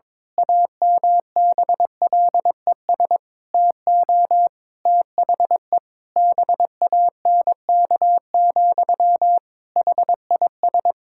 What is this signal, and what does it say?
Morse code, heard as a keyed tone, about 22 wpm, 700 Hz. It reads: AMBLES TO THE BANK, HIS
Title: Pangram Challenge v2 - 22wpm